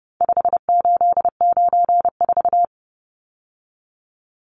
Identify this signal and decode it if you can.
Morse code, heard as a keyed tone, about 30 wpm, 700 Hz. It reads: 5894
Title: Numbers - 4-Digit 30wpm